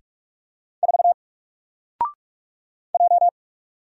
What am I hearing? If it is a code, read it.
Morse code, heard as a keyed tone, about 45 wpm, 700 Hz. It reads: 4 J
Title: Single Letter-Number Alt 45wpm